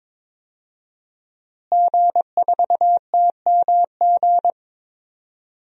Morse code, heard as a keyed tone, about 21 wpm, 700 Hz. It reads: G4TMG
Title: DXCC-100 Call Signs - 5 Characters - Repeated 2x - 22wpm